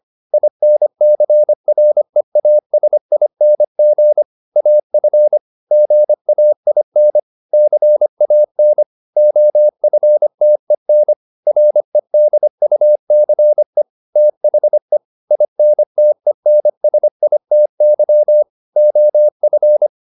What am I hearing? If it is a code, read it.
Morse code, heard as a keyed tone, about 25 wpm, 600 Hz. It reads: INCREASING AF GAIN CAN OFTEN REDUCE THE INTENSITY OF